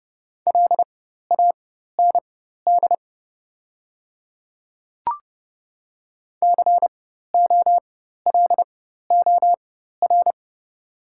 Morse code, heard as a keyed tone, about 32 wpm, spaced as for 14 wpm, 700 Hz. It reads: LAND COLOR
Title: 4x Character Spacing - Top 300 Words - 30wpm